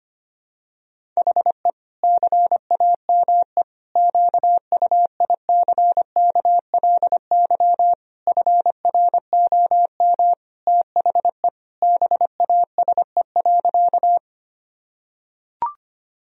Morse code, heard as a keyed tone, about 25 wpm, 700 Hz. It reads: HE CAME QUICKLY FROM THE BASE.
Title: Sentences from Top 1000 Words - Review Part 1 - 25wpm